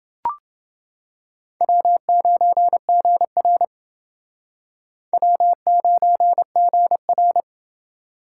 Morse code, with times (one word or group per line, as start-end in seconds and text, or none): 1.6-3.6 W9GR
5.1-7.4 W9GR